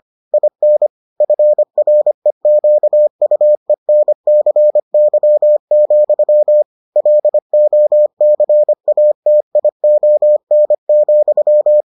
Morse code, heard as a keyed tone, about 25 wpm, 600 Hz. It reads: IN FREQUENCY, LOCATION,